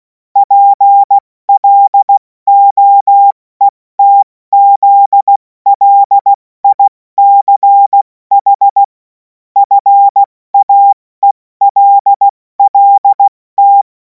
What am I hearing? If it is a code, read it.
Morse code, heard as a keyed tone, about 16 wpm, spaced as for 14 wpm, 800 Hz. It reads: PLOETZLICH FAELLT
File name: Morsekurs / Witze / Witz 07. April 2026 70 F80